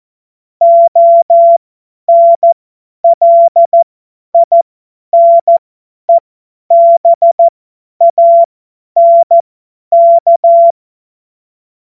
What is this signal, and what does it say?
Morse code, heard as a keyed tone, about 14 wpm, spaced as for 10 wpm, 680 Hz. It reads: ONLINEBANK
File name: Morsekurs / Witze / Witz 17 Juni 2025 50 F70